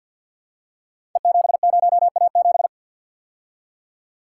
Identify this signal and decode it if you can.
Morse code, heard as a keyed tone, about 51 wpm, 700 Hz. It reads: E70A7
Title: Code Groups - 5 Letter:Numbers - 50wpm